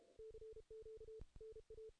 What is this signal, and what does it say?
Morse code, heard as a keyed tone, about 32 wpm, 440 Hz. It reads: CQ NA
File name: 2016 JAN CW NAQP / 2016 Jan naqp k5ot - 0-8